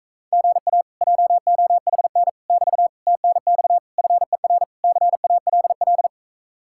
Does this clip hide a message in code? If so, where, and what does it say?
Yes: Morse code from 0.3 to 6.1 s, heard as a keyed tone, about 45 wpm, 700 Hz. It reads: GA JOHN <BT> TNX FER CALL